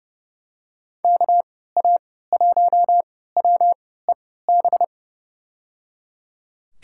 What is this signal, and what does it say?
Morse code, heard as a keyed tone, about 30 wpm, spaced as for 17 wpm, 700 Hz. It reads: KA1WEB
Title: USA Call Signs - 3x Character Spacing - 30wpm